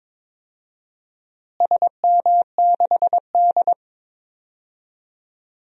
Morse code, heard as a keyed tone, about 22 wpm, 700 Hz. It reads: SM6D
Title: DXCC-100 Call Signs - 4 Characters - Repeated 2x - 22wpm